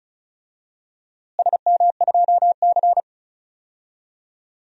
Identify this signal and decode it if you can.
Morse code, heard as a keyed tone, about 34 wpm, 700 Hz. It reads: SM2C
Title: DXCC-100 Call Signs - 4 Characters - Repeated 2x - 35wpm